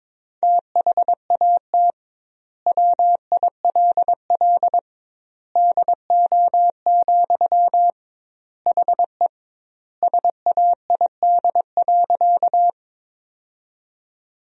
Morse code, heard as a keyed tone, about 22 wpm, 700 Hz. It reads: THAT WILL DO, HE SAID.
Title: 2x Word-Spacing - Sentences from Top 100 Words - 22wpm